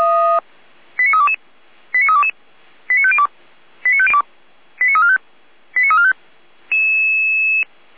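Five-tone selective call (ZVEI), heard as quick runs of five tones, tones of about 70 ms. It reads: 89322 89322 89502 89345 89345
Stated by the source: ZVEI five-tone call received on SDR.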